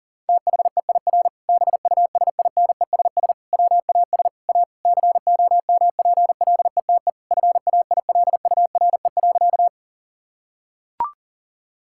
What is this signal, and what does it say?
Morse code, heard as a keyed tone, about 37 wpm, 700 Hz. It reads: THEIR BUSINESS WAS A COMPLETE FAILURE.
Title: Sentences from Top 1000 Words - Review Part 6 - 40wpm